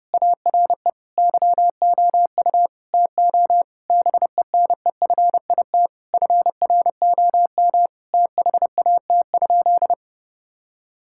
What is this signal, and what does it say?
Morse code, heard as a keyed tone, about 30 wpm, 700 Hz. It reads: ARE YOU TO BENEFIT FROM THAT?
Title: Sentences from Top 1000 Words - Review Part 6 - 30wpm